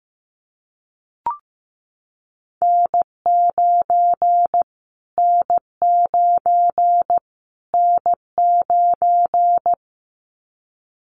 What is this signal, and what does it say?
Morse code, heard as a keyed tone, about 15 wpm, 700 Hz. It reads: N9 N9 N9
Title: Callsign Prefixes for Top 25 DXCC - Repeated 3x - 15wpm